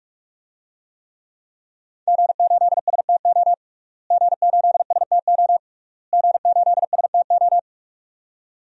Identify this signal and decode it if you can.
Morse code, heard as a keyed tone, about 46 wpm, 700 Hz. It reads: G8STO G8STO G8STO
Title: DXCC-100 Call Signs - 5 Characters - Repeated 3x - 45wpm